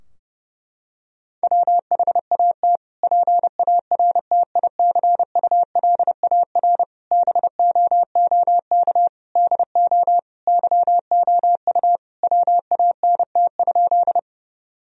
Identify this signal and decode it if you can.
Morse code, heard as a keyed tone, about 30 wpm, 700 Hz. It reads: WHAT PARTICULAR BOOK DO YOU WANT?
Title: Sentences from Top 1000 Words - Review Part 4 - 30wpm